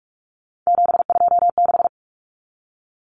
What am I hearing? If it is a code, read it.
Morse code, heard as a keyed tone, about 44 wpm, 700 Hz. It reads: 726
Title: Common Cut Numbers - 3-Digit 45wpm